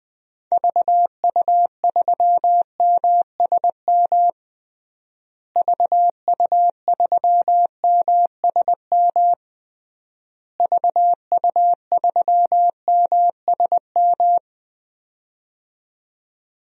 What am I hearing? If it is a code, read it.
Morse code, heard as a keyed tone, about 20 wpm, 700 Hz. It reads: VU3MSM VU3MSM VU3MSM
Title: DXCC-100 Call Signs - 6 Characters - Repeated 3x - 20wpm